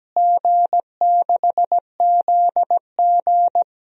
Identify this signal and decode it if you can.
Morse code, heard as a keyed tone, about 17 wpm, 700 Hz. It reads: G6ZG
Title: DXCC-100 Call Signs - 4 Characters - Repeated 3x - 17wpm